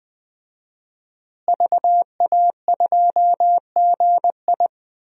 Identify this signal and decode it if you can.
Morse code, heard as a keyed tone, about 21 wpm, 700 Hz. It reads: VA2GI
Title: DXCC-100 Call Signs - 5 Characters - Repeated 2x - 20wpm